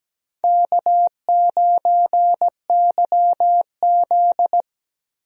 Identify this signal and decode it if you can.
Morse code, heard as a keyed tone, about 17 wpm, 700 Hz. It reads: K9YZ
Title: USA Call Signs - 4 Characters - 17wpm